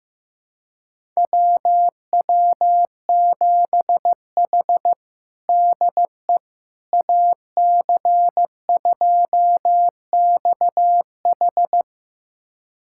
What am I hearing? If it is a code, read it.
Morse code, heard as a keyed tone, about 15 wpm, 700 Hz. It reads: WW7H DE AC2XH
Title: USA Callsign DE Callsign 15wpm